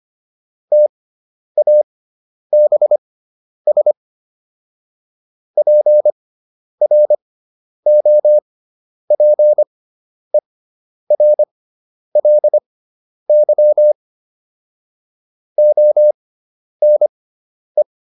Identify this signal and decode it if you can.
Morse code, heard as a keyed tone, about 25 wpm, spaced as for 10 wpm, 600 Hz. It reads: TABS PROPERLY ONE